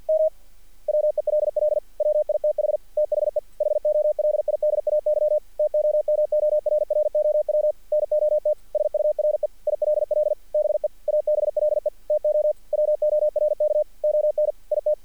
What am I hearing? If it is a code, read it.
Morse code, heard as a keyed tone, about 48 wpm, 600 Hz. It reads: M WELL WITH THE HOLIDAY TOMORROW NOT SURE ILL BE ABLE TO WORK ON IT
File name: archived / CRSnet